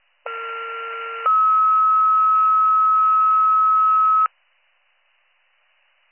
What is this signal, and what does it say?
Two-tone page, heard as a one-second tone, then a three-second tone, about 1370 then 1290 Hz. It is heard from 0.3 to 4.3 s.